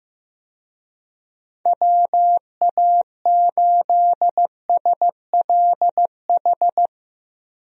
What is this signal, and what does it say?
Morse code, heard as a keyed tone, about 15 wpm, 700 Hz. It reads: WA8SLH
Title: USA Call Signs - 6 Characters - Repeated 3x - 15wpm